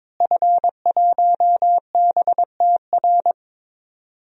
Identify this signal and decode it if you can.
Morse code, heard as a keyed tone, about 22 wpm, 700 Hz. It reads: F1BTR
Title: DXCC-100 Call Signs - 5 Characters - Repeated 3x - 22wpm